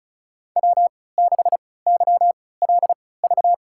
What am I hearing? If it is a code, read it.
Morse code, heard as a keyed tone, about 36 wpm, spaced as for 20 wpm, 700 Hz. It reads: W6YLV
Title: USA Call Signs - 5 Characters - 3x Character Spacing - 35wpm